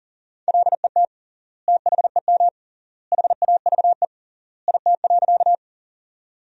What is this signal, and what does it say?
Morse code, heard as a keyed tone, about 40 wpm, 700 Hz. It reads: LET THEM HAVE IT.
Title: 3x Word-Spacing - Sentences from Top 300 Words - 40wpm